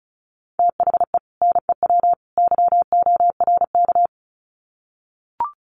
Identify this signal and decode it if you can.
Morse code, heard as a keyed tone, about 34 wpm, 700 Hz. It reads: THE NEW YORK
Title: Sets of 3 Words 35wpm